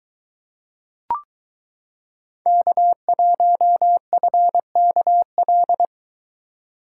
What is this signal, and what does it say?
Morse code, heard as a keyed tone, about 22 wpm, 700 Hz. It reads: K1FKL